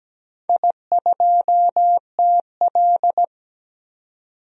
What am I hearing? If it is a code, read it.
Morse code, heard as a keyed tone, about 17 wpm, 700 Hz. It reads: I2TL
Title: DXCC-100 Call Signs - 4 Characters - 17wpm